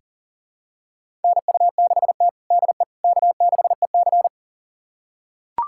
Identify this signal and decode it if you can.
Morse code, heard as a keyed tone, about 40 wpm, 700 Hz. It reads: NU6T DE K6EC
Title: USA Callsign DE Callsign - 4 Characters - 40wpm